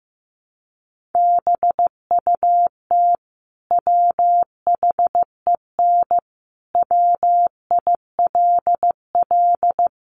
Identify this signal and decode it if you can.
Morse code, heard as a keyed tone, about 15 wpm, 700 Hz. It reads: BUT WHEN WILL
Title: Sentences from Top 100 Words 15wpm